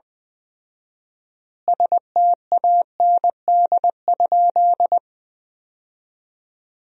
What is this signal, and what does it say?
Morse code, heard as a keyed tone, about 20 wpm, 700 Hz. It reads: STAND?
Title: 4x Word-Spacing - Sentences from Top 300 Words - 20wpm